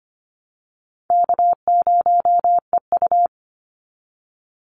Morse code, heard as a keyed tone, about 25 wpm, 700 Hz. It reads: K0EU
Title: Easy Call Signs Repeated 2x - 25wpm